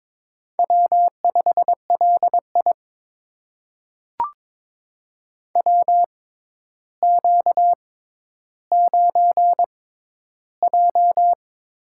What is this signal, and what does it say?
Morse code, heard as a keyed tone, about 22 wpm, spaced as for 11 wpm, 700 Hz. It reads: W5LI WQ9J
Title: USA Call Signs - 6x Character Spacing - 22wpm